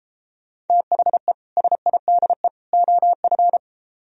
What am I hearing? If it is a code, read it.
Morse code, heard as a keyed tone, about 33 wpm, 700 Hz. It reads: THE SIDE OF